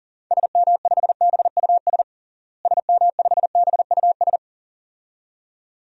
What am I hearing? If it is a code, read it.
Morse code, heard as a keyed tone, about 40 wpm, 700 Hz. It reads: SM5BUS SM5BUS
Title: DXCC-100 Call Signs - 6 Characters - Repeated 2x - 40wpm